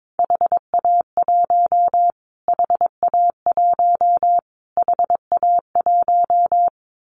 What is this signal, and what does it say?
Morse code, heard as a keyed tone, about 22 wpm, 700 Hz. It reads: HA1 HA1 HA1
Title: Callsign Prefixes for Top 25 DXCC - Repeated 3x - 22wpm